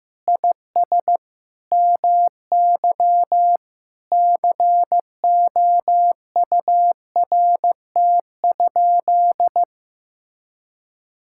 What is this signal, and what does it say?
Morse code, heard as a keyed tone, about 14 wpm, 700 Hz. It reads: IS MY COURT?
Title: Sentences from Top 1600 Words 15wpm